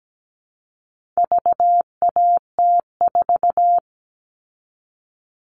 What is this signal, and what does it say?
Morse code, heard as a keyed tone, about 17 wpm, 700 Hz. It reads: VAT4
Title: All Cut Numbers - 4-Digit 17wpm